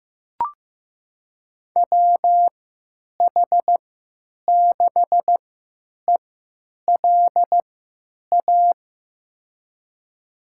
Morse code, heard as a keyed tone, about 15 wpm, spaced as for 9 wpm, 700 Hz. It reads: WH6ELA
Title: USA Call Signs - 6 Characters - 3x Character Spacing - 15wpm